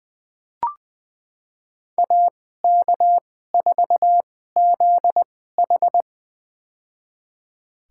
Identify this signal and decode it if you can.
Morse code, heard as a keyed tone, about 20 wpm, spaced as for 14 wpm, 700 Hz. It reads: AK4ZH
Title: USA Call Signs - 5 Characters - 2x Character Spacing - 20wpm